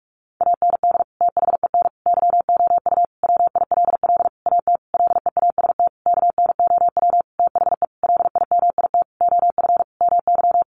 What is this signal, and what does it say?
Morse code, heard as a keyed tone, about 44 wpm, 700 Hz. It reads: AND THEN YOU WILL AT LEAST KNOW THE LIMIT OF MY
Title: Sentences from Top 1500 Words 45wpm